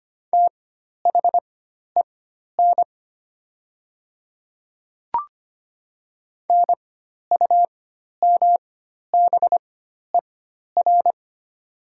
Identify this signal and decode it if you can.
Morse code, heard as a keyed tone, about 25 wpm, spaced as for 12 wpm, 700 Hz. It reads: THEN NUMBER
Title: 4x Character Spacing - Top 100 Words - 25wpm